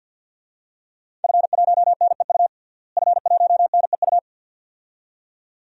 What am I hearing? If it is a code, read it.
Morse code, heard as a keyed tone, about 52 wpm, 700 Hz. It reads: F1NEU F1NEU
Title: DXCC-100 Call Signs - 5 Characters - Repeated 2x - 50wpm